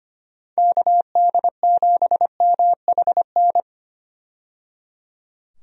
Morse code, heard as a keyed tone, about 25 wpm, 700 Hz. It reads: KD7MHN
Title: USA Call Signs - 6 Characters - 25wpm